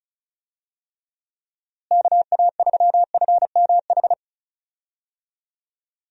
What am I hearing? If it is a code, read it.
Morse code, heard as a keyed tone, about 36 wpm, 700 Hz. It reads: KA3FMH
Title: USA Call Signs - 6 Characters - Repeated 2x - 35wpm